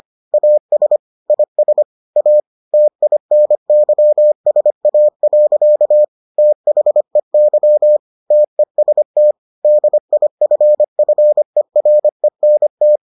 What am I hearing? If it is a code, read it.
Morse code, heard as a keyed tone, about 25 wpm, 600 Hz. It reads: AS IS A TINYSA. THEY TEST DIFFERENT